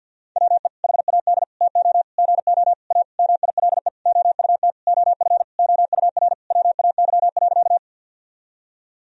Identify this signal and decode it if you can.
Morse code, heard as a keyed tone, about 50 wpm, 700 Hz. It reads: WE HAD TO GO A MILE OUT OF OUR WAY.
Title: Sentences from Top 1000 Words - Review Part 1 - 50wpm